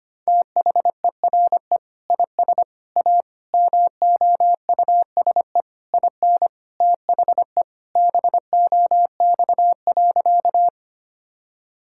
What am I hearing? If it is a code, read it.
Morse code, heard as a keyed tone, about 25 wpm, 700 Hz. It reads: THERE IS A MOUSE IN THE BOX.
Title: Sentences from Top 1000 Words - Review Part 5 - 25wpm